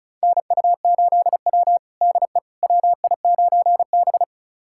Morse code, heard as a keyed tone, about 34 wpm, 700 Hz. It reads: NU8W DE WI9B
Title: USA Callsign DE Callsign - 4 Characters - 35wpm